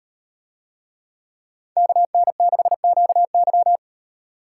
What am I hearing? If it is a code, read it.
Morse code, heard as a keyed tone, about 38 wpm, 700 Hz. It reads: KN6QY